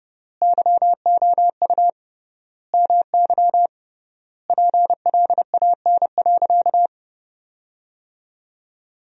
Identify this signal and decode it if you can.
Morse code, heard as a keyed tone, about 30 wpm, 700 Hz. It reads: YOU MY PLAN.
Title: 3x Word-Spacing - Sentences from Top 500 Words - 30wpm